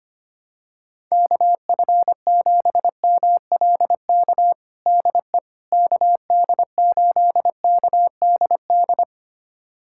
Morse code, heard as a keyed tone, about 25 wpm, 700 Hz. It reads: KF7MLK DE KD8KDD
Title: USA Callsign DE Callsign 25wpm